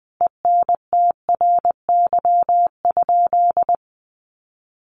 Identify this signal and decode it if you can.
Morse code, heard as a keyed tone, about 19 wpm, 700 Hz. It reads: ENTRY?
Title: Sentences from Top 1500 Words 20wpm